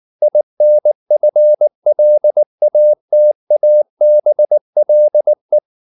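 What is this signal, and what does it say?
Morse code, heard as a keyed tone, about 19 wpm, 600 Hz. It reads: INFLATABLE
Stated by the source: Morse code at 19 wpm from CW med Gnister, reading Ken Follett's novel Never in English.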